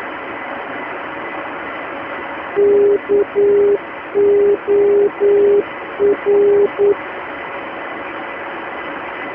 Morse code, heard as a keyed tone, about 9 wpm, 410 Hz, with noise. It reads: KOR
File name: uploader / 322 khz